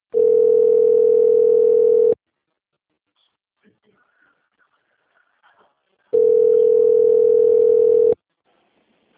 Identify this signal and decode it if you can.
Ringback tone (North America), 440 + 480 Hz, with speech.
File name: rg-1600-unknown- - - .9840